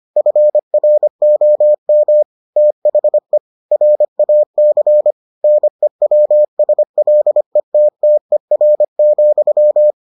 Morse code, heard as a keyed tone, about 25 wpm, 600 Hz. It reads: FROM THE RAC NEWSLETTER,